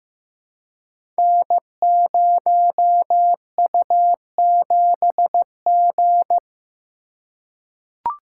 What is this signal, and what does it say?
Morse code, heard as a keyed tone, about 15 wpm, 700 Hz. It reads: N0U7G
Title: Code Groups - 5 Letter:Numbers - 15wpm